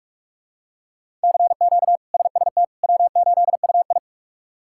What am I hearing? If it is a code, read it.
Morse code, heard as a keyed tone, about 43 wpm, 700 Hz. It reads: CQ SST W8UI
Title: K1USN SST QSO Practice - 45wpm